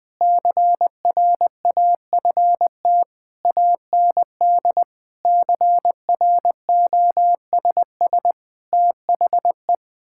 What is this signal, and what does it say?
Morse code, heard as a keyed tone, about 21 wpm, 700 Hz. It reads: CRAFT AND CROSS THE
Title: Sentences from Top 1000 Words - Review Part 6 - 20wpm